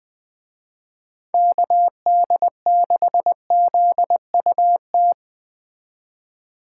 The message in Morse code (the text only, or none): KD6ZUT